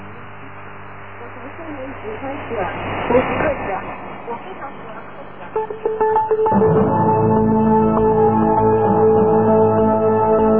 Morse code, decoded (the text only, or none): none